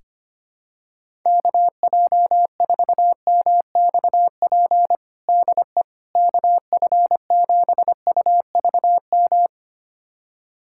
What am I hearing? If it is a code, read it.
Morse code, heard as a keyed tone, about 25 wpm, 700 Hz. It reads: KJ4MXP DE KF7UVM
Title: USA Callsign DE Callsign 25wpm